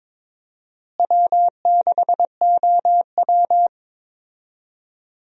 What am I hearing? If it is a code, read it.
Morse code, heard as a keyed tone, about 21 wpm, 700 Hz. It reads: W6OW
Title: USA Call Signs - 4 Characters - Repeated 3x - 22wpm